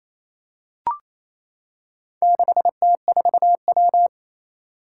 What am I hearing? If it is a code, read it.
Morse code, heard as a keyed tone, about 27 wpm, 700 Hz. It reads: 6T4W
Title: Four Challenging Letter-Number Combinations 28wpm